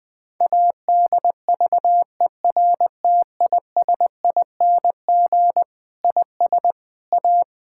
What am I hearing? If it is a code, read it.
Morse code, heard as a keyed tone, about 20 wpm, 700 Hz. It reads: ADVERTISING IS A
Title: Sentences from Top 1000 Words - Review Part 6 - 20wpm